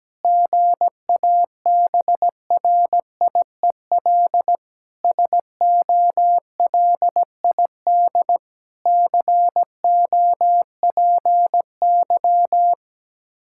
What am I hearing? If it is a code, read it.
Morse code, heard as a keyed tone, about 17 wpm, 700 Hz. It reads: GABRIEL SOLID COPY
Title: Practice QSO - Exchange 3 - More Info - 17wpm